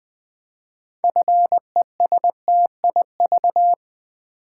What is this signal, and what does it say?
Morse code, heard as a keyed tone, about 20 wpm, 700 Hz. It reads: FESTIV
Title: Build A Word - 3 Letters At A Time - 8-Letter Words - 20wpm